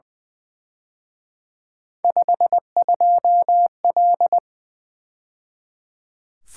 Morse code, heard as a keyed tone, about 24 wpm, 700 Hz. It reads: 52L
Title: Speed-Racing - Three Letter-Number Combinations - 20wpm